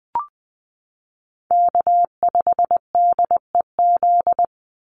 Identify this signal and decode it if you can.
Morse code, heard as a keyed tone, about 20 wpm, 700 Hz. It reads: K5DEZ
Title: NA SOTA Chaser Callsigns 20wpm